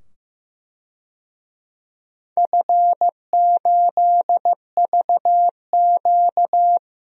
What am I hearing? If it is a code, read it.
Morse code, heard as a keyed tone, about 15 wpm, 700 Hz. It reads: F8VQ
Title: DXCC-100 Call Signs - 4 Characters - 15wpm